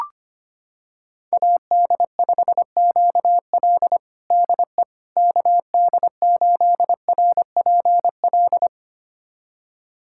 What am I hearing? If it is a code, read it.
Morse code, heard as a keyed tone, about 25 wpm, 700 Hz. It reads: AD5QL DE KD8RPL
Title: USA Callsign DE Callsign 25wpm